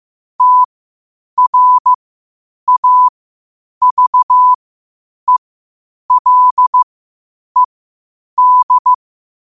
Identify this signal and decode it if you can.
Morse code, heard as a keyed tone, about 15 wpm, spaced as for 8 wpm, 1000 Hz. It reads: TRAVELED
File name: qotdfiles / 8wpm